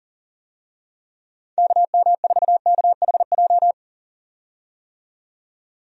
Morse code, heard as a keyed tone, about 39 wpm, 700 Hz. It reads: KM4KHJ
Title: USA Call Signs - 6 Characters - 40wpm